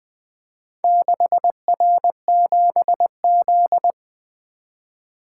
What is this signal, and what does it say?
Morse code, heard as a keyed tone, about 20 wpm, 700 Hz. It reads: 6R7Z
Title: Four Letter-Number Combinations - 20wpm